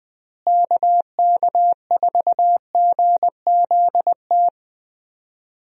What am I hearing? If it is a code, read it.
Morse code, heard as a keyed tone, about 20 wpm, 700 Hz. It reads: KK4GZT
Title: USA Call Signs - 6 Characters - Repeated 3x - 20wpm